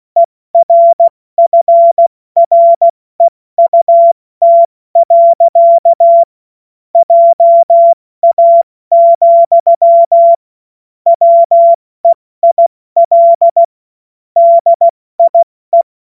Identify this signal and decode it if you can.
Morse code, heard as a keyed tone, about 16 wpm, 680 Hz. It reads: ERFREUT. JA, WEIL DIE
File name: Morsekurs / Witze / Witz 16. Mai 2023 70 F80